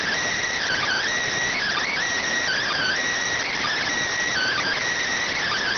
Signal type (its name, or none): Emergency Alert System (SAME) tones